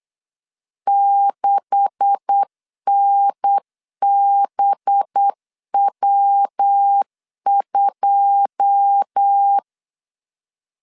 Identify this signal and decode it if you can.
Morse code, heard as a keyed tone, about 8 wpm, 800 Hz. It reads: 6NBW2
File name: CW / DET / 1 9 1500C S C 8wpm 5653